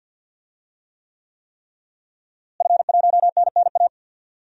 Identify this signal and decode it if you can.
Morse code, heard as a keyed tone, about 52 wpm, 700 Hz. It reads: F1NNA
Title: DXCC-100 Call Signs - 5 Characters - Repeated 3x - 50wpm